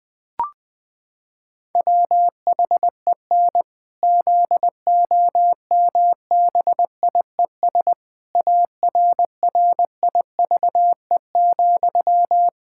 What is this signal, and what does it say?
Morse code, heard as a keyed tone, about 20 wpm, 700 Hz. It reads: WHEN ZOMBIES ARRIVE,
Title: Pangram Challenge v2 - 20wpm